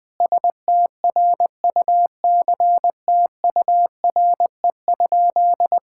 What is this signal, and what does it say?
Morse code, heard as a keyed tone, about 21 wpm, 700 Hz. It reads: STRUCTURE?
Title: Sentences from Top 1000 Words - Review Part 6 - 20wpm